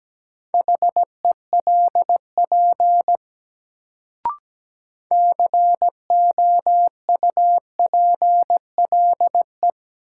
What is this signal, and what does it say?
Morse code, heard as a keyed tone, about 17 wpm, 700 Hz. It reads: HELP COUPLE
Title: Sets of 2 Words - New Word Drill - 17wpm